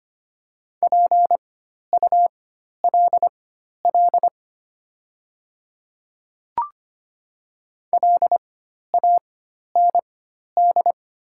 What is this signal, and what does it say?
Morse code, heard as a keyed tone, about 25 wpm, spaced as for 12 wpm, 690 Hz. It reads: PULL LAND
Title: 4x Character Spacing - Top 500 Words - 25wpm